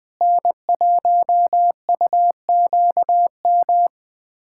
Morse code, heard as a keyed tone, about 20 wpm, 700 Hz. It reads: N1UQM
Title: USA Callsign DE Callsign - 5 Characters - 20wpm